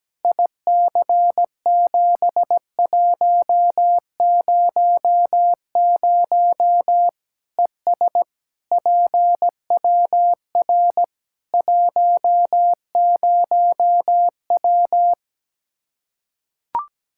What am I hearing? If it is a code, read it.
Morse code, heard as a keyed tone, about 17 wpm, 700 Hz. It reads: IC7100 ES PWR 10W
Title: K1USN SST QSO Practice 17wpm